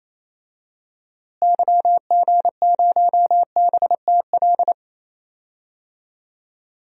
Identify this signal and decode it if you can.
Morse code, heard as a keyed tone, about 28 wpm, 700 Hz. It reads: YG0BTL
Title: DXCC-100 Call Signs - 6 Characters - Repeated 2x - 28wpm